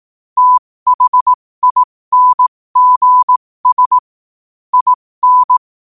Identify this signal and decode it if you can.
Morse code, heard as a keyed tone, about 18 wpm, spaced as for 15 wpm, 1000 Hz. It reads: THINGS IN
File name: qotdfiles / 15wpm